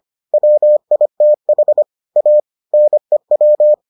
Morse code, heard as a keyed tone, about 25 wpm, 600 Hz. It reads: WITH A NEW